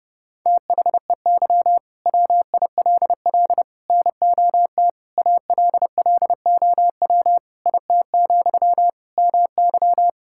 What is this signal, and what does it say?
Morse code, heard as a keyed tone, about 32 wpm, 700 Hz. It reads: THEY WILL NOT ALLOW IT, MY
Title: Sentences from Top 1000 Words - Review Part 5 - 30wpm